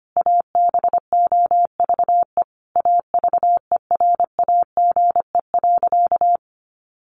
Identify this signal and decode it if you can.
Morse code, heard as a keyed tone, about 25 wpm, 700 Hz. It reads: ABOVE AVERAGE.
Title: Sentences from Top 1400 Words 25wpm